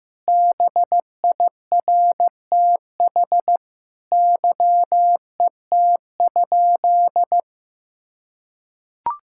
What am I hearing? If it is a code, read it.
Morse code, heard as a keyed tone, about 15 wpm, 700 Hz. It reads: BIRTH YET?
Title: Sentences from Top 1000 Words - Review Part 5 - 15wpm